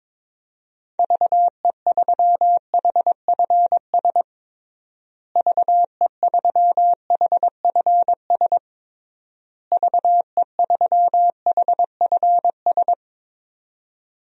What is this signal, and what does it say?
Morse code, heard as a keyed tone, about 22 wpm, 700 Hz. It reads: VE3HFS VE3HFS VE3HFS
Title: DXCC-100 Call Signs - 6 Characters - Repeated 3x - 22wpm